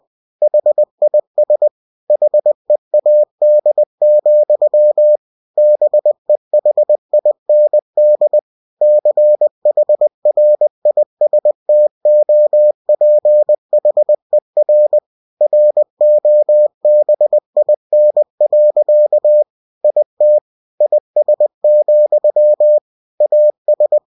Morse code, heard as a keyed tone, about 20 wpm, 600 Hz. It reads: HIS HEAD, BEHIND CHRISTOPHER ROBIN. IT IS, AS
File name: cw-books / winnie-the-pooh / winnie-20wpm-page-0000